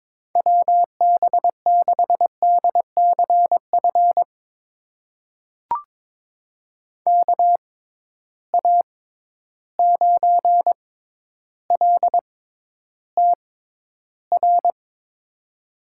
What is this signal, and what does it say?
Morse code, heard as a keyed tone, about 22 wpm, spaced as for 11 wpm, 700 Hz. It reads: WB6DCF KA9LTR
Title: USA Call Signs - 6 Characters - 6x Character Spacing - 22wpm